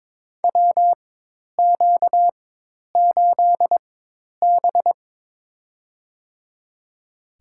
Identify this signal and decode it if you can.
Morse code, heard as a keyed tone, about 22 wpm, spaced as for 10 wpm, 700 Hz. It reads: WQ8B
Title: USA Call Signs - 4x Character Spacing - 22wpm